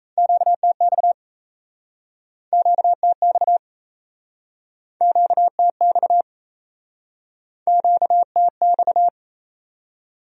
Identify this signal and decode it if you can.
Morse code, heard as a keyed tone, about 39 wpm, 700 Hz. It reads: QTX QTX QTX QTX